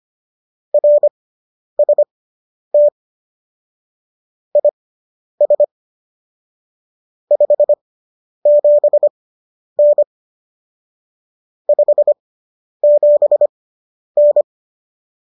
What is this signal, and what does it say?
Morse code, heard as a keyed tone, about 25 wpm, spaced as for 10 wpm, 590 Hz. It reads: RST IS 57N 57N